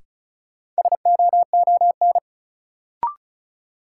Morse code, heard as a keyed tone, about 36 wpm, 700 Hz. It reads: SOON
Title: Sentences from Top 400 Words - New Word Drill - 35wpm